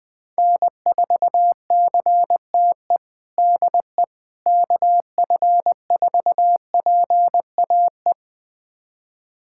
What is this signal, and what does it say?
Morse code, heard as a keyed tone, about 20 wpm, 700 Hz. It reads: N4CTE DE KF4PAE
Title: USA Callsign DE Callsign 20wpm